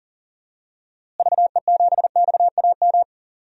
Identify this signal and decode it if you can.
Morse code, heard as a keyed tone, about 39 wpm, 700 Hz. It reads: VE7XAM
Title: DXCC-100 Call Signs - 6 Characters - Repeated 3x - 40wpm